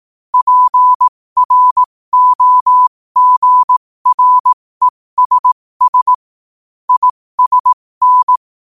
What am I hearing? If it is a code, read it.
Morse code, heard as a keyed tone, about 18 wpm, spaced as for 15 wpm, 1000 Hz. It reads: PROGRESS ISN
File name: qotdfiles / 15wpm